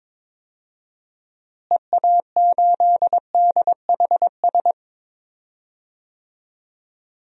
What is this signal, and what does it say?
Morse code, heard as a keyed tone, about 22 wpm, 700 Hz. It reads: EA8DHS
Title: DXCC-100 Call Signs - 6 Characters - Repeated 2x - 22wpm